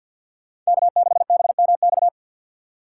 Morse code, heard as a keyed tone, about 47 wpm, 700 Hz. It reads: K6BMX
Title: USA Call Signs - 6x Character Spacing - 50wpm